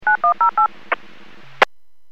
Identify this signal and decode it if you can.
DTMF, heard as telephone touch tones, tones of about 110 ms. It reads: 9108
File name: WPS / 00018 07-39 WPS 451.25 WPS